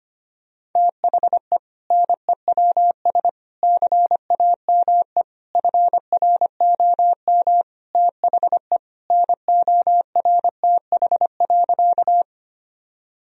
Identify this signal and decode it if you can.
Morse code, heard as a keyed tone, about 25 wpm, 700 Hz. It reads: THE NEWS CAME FROM THE NORTH.
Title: Sentences from Top 1000 Words - Review Part 1 - 25wpm